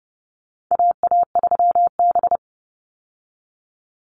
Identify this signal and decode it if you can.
Morse code, heard as a keyed tone, about 30 wpm, 700 Hz. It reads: AA3B
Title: Call Signs Repeated 2x - 30wpm